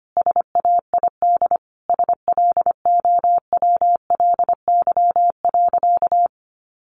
Morse code, heard as a keyed tone, about 25 wpm, 700 Hz. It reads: SAID SLOWLY.
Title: Sentences from Top 1100 Words 25wpm